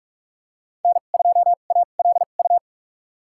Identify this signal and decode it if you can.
Morse code, heard as a keyed tone, about 46 wpm, spaced as for 32 wpm, 700 Hz. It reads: N2ALU
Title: USA Call Signs - 5 Characters - 2x Character Spacing - 45wpm